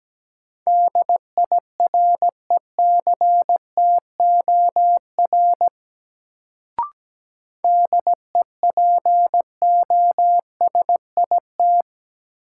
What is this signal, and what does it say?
Morse code, heard as a keyed tone, about 17 wpm, 700 Hz. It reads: DIRECTOR DEPOSIT
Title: Sentences from Top 1300 Words - New Word Drill Alt - 17wpm